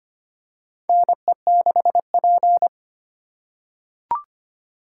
Morse code, heard as a keyed tone, about 25 wpm, 700 Hz. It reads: NE6P
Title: USA Call Signs - 4 Characters - 2x Character Spacing - 25wpm